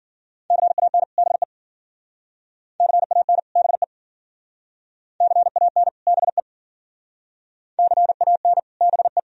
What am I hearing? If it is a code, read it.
Morse code, heard as a keyed tone, about 52 wpm, 700 Hz. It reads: CAN BE CAN BE CAN BE CAN BE